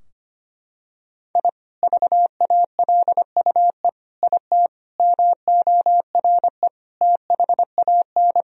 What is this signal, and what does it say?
Morse code, heard as a keyed tone, about 25 wpm, 700 Hz. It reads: I VALUE IT MORE THAN
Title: Sentences from Top 1000 Words - Review Part 3 - 25wpm